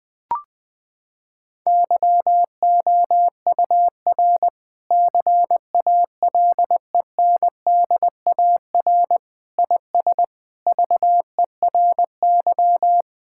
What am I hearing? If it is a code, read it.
Morse code, heard as a keyed tone, about 20 wpm, 700 Hz. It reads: YOUR CALENDAR IS VERY
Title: Sentences from Top 1600 Words 20wpm